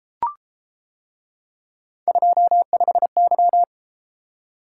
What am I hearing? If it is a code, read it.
Morse code, heard as a keyed tone, about 31 wpm, 700 Hz. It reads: 25Y